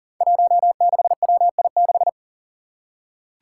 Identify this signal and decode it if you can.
Morse code, heard as a keyed tone, about 40 wpm, 700 Hz. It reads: 16WI6
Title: Five Challenging Letter-Number Combinations 40wpm